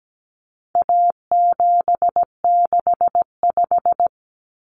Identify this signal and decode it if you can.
Morse code, heard as a keyed tone, about 17 wpm, 700 Hz. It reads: A765
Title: Common Cut Numbers - 4-Digit 17wpm